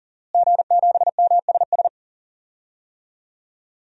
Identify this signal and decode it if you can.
Morse code, heard as a keyed tone, about 40 wpm, 700 Hz. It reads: G7MSS
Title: DXCC-100 Call Signs - 5 Characters - Repeated 2x - 40wpm